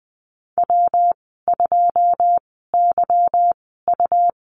Morse code, heard as a keyed tone, about 20 wpm, spaced as for 15 wpm, 700 Hz. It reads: W2YU
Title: 167 Lesson 27g - Number 7 - Call Signs 20wpm